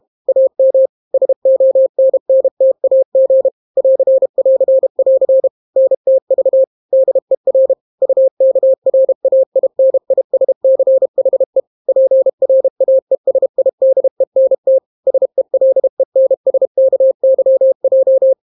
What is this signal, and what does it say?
Morse code, heard as a keyed tone, about 32 wpm, 530 Hz. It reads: AM SONNTAG <AR><AR><AR> NTV DER UKRAINISCHE PRAESIDENT SELENSKYJ